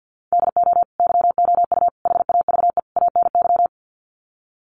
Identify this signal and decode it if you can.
Morse code, heard as a keyed tone, about 50 wpm, 700 Hz. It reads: DO YOU HAVE ANY
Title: Sets of 4 Words 50wpm